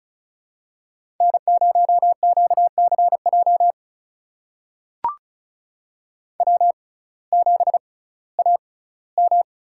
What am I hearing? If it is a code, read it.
Morse code, heard as a keyed tone, about 36 wpm, spaced as for 19 wpm, 700 Hz. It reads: N0QCJ W7AM
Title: USA Call Signs - 5 Characters - 6x Character Spacing - 35wpm